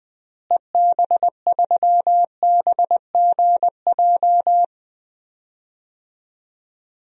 Morse code, heard as a keyed tone, about 21 wpm, 700 Hz. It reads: EB3BGJ
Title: DXCC-100 Call Signs - 6 Characters - Repeated 3x - 20wpm